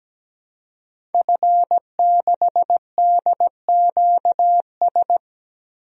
Morse code, heard as a keyed tone, about 17 wpm, 700 Hz. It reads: F6DQS
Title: DXCC-100 Call Signs - 5 Characters - 17wpm